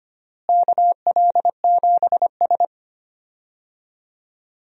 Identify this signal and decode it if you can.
Morse code, heard as a keyed tone, about 25 wpm, 700 Hz. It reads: KL7S
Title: DXCC-100 Call Signs - 4 Characters - Repeated 2x - 25wpm